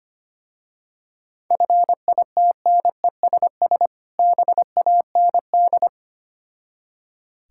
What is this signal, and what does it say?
Morse code, heard as a keyed tone, about 25 wpm, 700 Hz. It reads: FITNESS BAND
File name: practice / 25wpm / Difficult Words with Hints - 25wpm